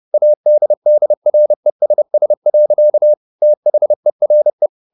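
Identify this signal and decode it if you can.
Morse code, heard as a keyed tone, about 30 wpm, 600 Hz. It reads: ADDRESS. THERE
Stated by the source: Morse code at 30 wpm, the novel Never by Ken Follett in English, from CW med Gnister.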